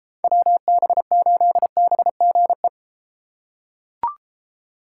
Morse code, heard as a keyed tone, about 33 wpm, 700 Hz. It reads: WB8BGE